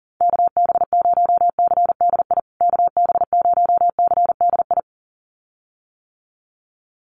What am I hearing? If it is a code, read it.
Morse code, heard as a keyed tone, about 40 wpm, 700 Hz. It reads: KB0CDI KB0CDI
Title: USA Call Signs Repeated 2x 40wpm